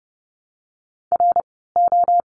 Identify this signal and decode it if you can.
Morse code, heard as a keyed tone, about 30 wpm, 700 Hz. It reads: R O
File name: Lesson 17f - Letter u - Words 30wpm